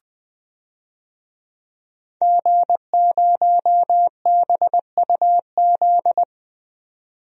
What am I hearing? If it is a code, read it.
Morse code, heard as a keyed tone, about 20 wpm, 700 Hz. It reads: G0BUZ
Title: DXCC-100 Call Signs - 5 Characters - 20wpm